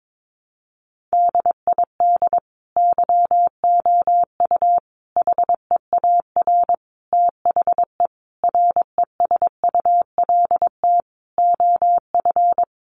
Morse code, heard as a keyed tone, about 22 wpm, 700 Hz. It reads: DID YOU HEAR THE RESULT OF
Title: Sentences from Top 1400 Words 22wpm